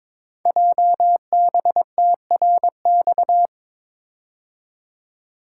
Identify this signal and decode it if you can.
Morse code, heard as a keyed tone, about 22 wpm, 700 Hz. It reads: JBTRX
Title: Code Groups - 5 Letters - 22wpm